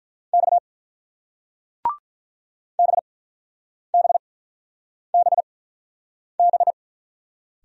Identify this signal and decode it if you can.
Morse code, heard as a keyed tone, about 48 wpm, spaced as for 9 wpm, 700 Hz. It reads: X BBBB